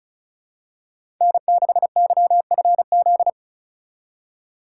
Morse code, heard as a keyed tone, about 36 wpm, 700 Hz. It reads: N6YFZ
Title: USA Call Signs - 8x Character Spacing - 35wpm